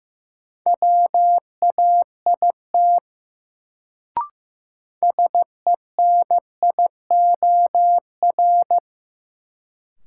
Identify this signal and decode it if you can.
Morse code, heard as a keyed tone, about 15 wpm, 700 Hz. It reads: WAIT SENIOR
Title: Sets of 5 Words Encore - New Word Drill - 15wpm